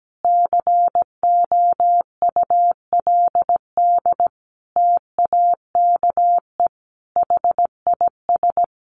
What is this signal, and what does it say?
Morse code, heard as a keyed tone, about 17 wpm, 700 Hz. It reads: COULD TAKE HIS
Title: Sentences from Top 900 Words 17wpm